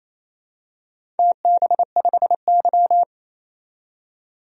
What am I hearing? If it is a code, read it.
Morse code, heard as a keyed tone, about 28 wpm, 700 Hz. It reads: TB5Y
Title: DXCC-100 Call Signs - 4 Characters - 28wpm